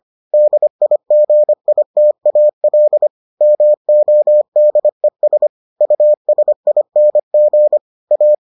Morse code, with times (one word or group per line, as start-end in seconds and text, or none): 0.3-3.1 DIGITAL
3.4-5.5 MODES
5.8-7.8 USING
8.1-8.3 A